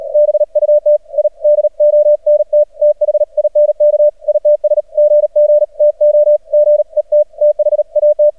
Morse code, heard as a keyed tone, about 40 wpm, 600 Hz. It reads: BUT I DONT THINK ITS GG TO GET THAT
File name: archived / CRSnet